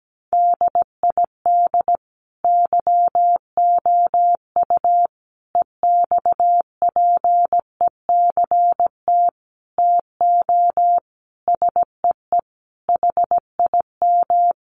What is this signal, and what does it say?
Morse code, heard as a keyed tone, about 17 wpm, 700 Hz. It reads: DID YOU EXPECT TO SEE HIM
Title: Sentences from Top 1100 Words 17wpm